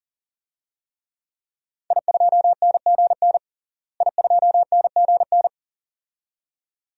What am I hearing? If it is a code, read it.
Morse code, heard as a keyed tone, about 40 wpm, 700 Hz. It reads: I2NGN I2NGN
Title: DXCC-100 Call Signs - 5 Characters - Repeated 2x - 40wpm